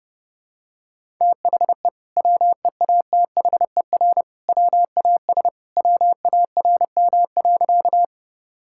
Morse code, heard as a keyed tone, about 28 wpm, 700 Hz. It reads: THE WEATHER WAS WARM.
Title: Sentences from Top 1000 Words - Review Part 5 - 30wpm